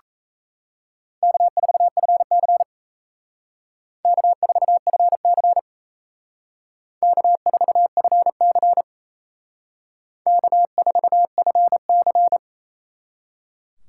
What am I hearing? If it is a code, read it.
Morse code, heard as a keyed tone, about 36 wpm, 700 Hz. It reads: K4FC K4FC K4FC K4FC